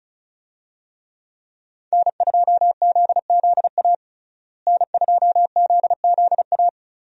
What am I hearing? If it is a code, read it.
Morse code, heard as a keyed tone, about 36 wpm, 700 Hz. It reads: N2ZZA N2ZZA
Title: USA Call Signs - 5 Characters - Repeated 2x - 35wpm